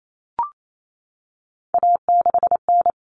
Morse code, heard as a keyed tone, about 28 wpm, 700 Hz. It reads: A6N
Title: All Cut Numbers - 3-Digit 28wpm